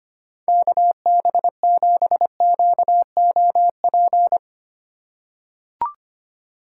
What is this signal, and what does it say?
Morse code, heard as a keyed tone, about 25 wpm, 700 Hz. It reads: KB7QOP
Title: K1USN SST QSO Practice - 25wpm